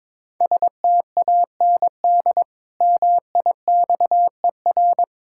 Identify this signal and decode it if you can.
Morse code, heard as a keyed tone, about 22 wpm, 700 Hz. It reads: STAND MIXER
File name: Easy Words - Speed-Racing with Hints - 22wpm